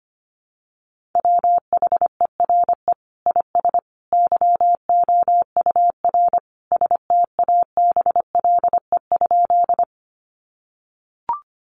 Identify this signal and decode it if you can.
Morse code, heard as a keyed tone, about 25 wpm, 700 Hz. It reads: WHERE IS YOUR STABLE?
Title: Sentences from Top 1500 Words 25wpm